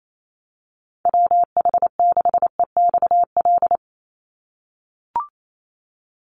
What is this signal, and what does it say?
Morse code, heard as a keyed tone, about 28 wpm, 700 Hz. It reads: WH6EXL
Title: USA Call Signs 28wpm